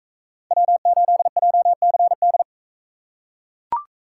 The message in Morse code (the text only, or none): W8JCD